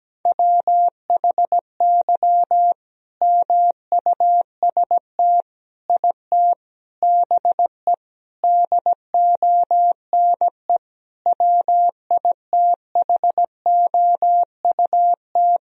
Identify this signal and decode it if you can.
Morse code, heard as a keyed tone, about 17 wpm, 700 Hz. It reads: WHY MUST IT BE DONE WITHOUT
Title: Sentences from Top 1000 Words - Review Part 3 - 17wpm